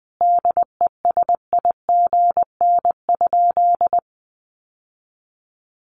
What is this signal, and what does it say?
Morse code, heard as a keyed tone, about 20 wpm, 700 Hz. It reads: DESIGN?
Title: Sentences from Top 800 Words 20wpm